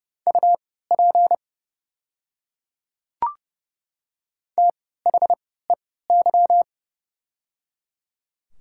Morse code, heard as a keyed tone, about 30 wpm, spaced as for 17 wpm, 700 Hz. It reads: UP THEY
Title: 3x Character Spacing - Top 100 Words - 30wpm